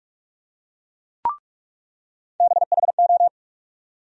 Morse code, heard as a keyed tone, about 47 wpm, 700 Hz. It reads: BHO
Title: USA Call Sign Postfixes - 3 Letters 45wpm